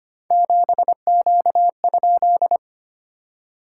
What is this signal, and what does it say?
Morse code, heard as a keyed tone, about 25 wpm, 700 Hz. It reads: 7Q?
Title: SOTA QSO Practice 25wpm